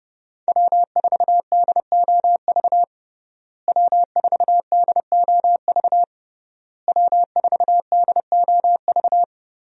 Morse code, heard as a keyed tone, about 30 wpm, 700 Hz. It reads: W4DOV W4DOV W4DOV
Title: USA Call Signs - 5 Characters - Repeated 3x - 30wpm